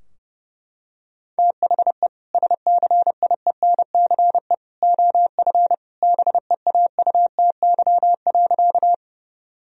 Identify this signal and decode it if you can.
Morse code, heard as a keyed tone, about 30 wpm, 700 Hz. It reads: THE SCIENCE OF BEAUTY.
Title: Sentences from Top 1000 Words - Review Part 2 - 30wpm